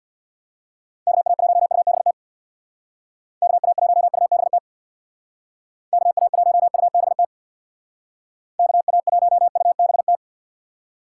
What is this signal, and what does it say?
Morse code, heard as a keyed tone, about 63 wpm, 700 Hz. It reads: KA1UBT KA1UBT KA1UBT KA1UBT